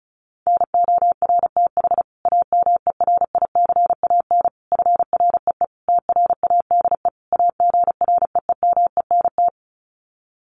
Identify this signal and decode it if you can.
Morse code, heard as a keyed tone, about 34 wpm, 700 Hz. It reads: NORTH AMERICAN FREE TRADE AGREEMENT
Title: Sets of 5 Words Encore 35wpm